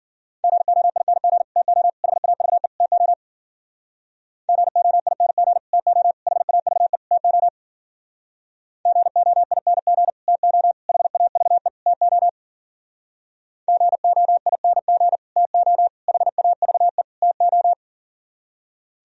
Morse code, heard as a keyed tone, about 50 wpm, 700 Hz. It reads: GOING TO HAVE TO GOING TO HAVE TO GOING TO HAVE TO GOING TO HAVE TO